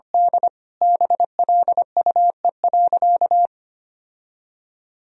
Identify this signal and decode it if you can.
Morse code, heard as a keyed tone, about 25 wpm, 700 Hz. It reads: D BLUE.
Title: Sentences from Top 1000 Words - Review Part 3 - 25wpm